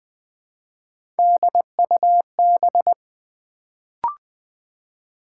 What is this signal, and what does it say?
Morse code, heard as a keyed tone, about 20 wpm, 700 Hz. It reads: DUB
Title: Build A Word - 1 Letter At A Time - 3-Letter Words - 20wpm